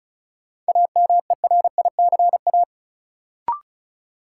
Morse code, heard as a keyed tone, about 36 wpm, 700 Hz. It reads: AMERICA
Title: Sets of 5 Words - New Word Drill - 35wpm